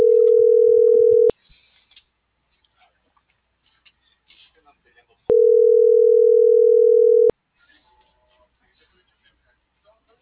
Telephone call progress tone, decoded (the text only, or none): ringback tone